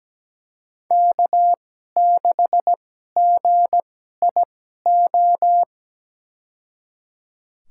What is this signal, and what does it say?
Morse code, heard as a keyed tone, about 17 wpm, spaced as for 12 wpm, 700 Hz. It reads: K6GIO
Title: USA Call Signs - 5 Characters - 2x Character Spacing - 17wpm